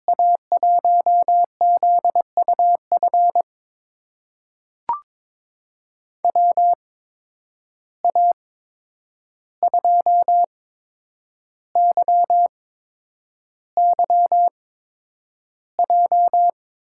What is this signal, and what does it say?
Morse code, heard as a keyed tone, about 21 wpm, spaced as for 9 wpm, 700 Hz. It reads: A1ZUF WA2YYJ
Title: USA Call Signs - 6 Characters - 8x Character Spacing - 22wpm